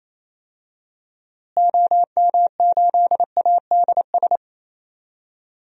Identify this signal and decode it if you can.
Morse code, heard as a keyed tone, about 28 wpm, 700 Hz. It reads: OM8ADS
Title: DXCC-100 Call Signs - 6 Characters - Repeated 3x - 28wpm